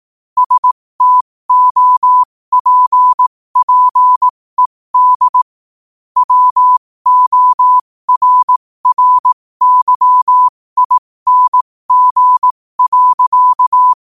Morse code, heard as a keyed tone, about 18 wpm, spaced as for 15 wpm, 1000 Hz. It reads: STOPPED WORRYING.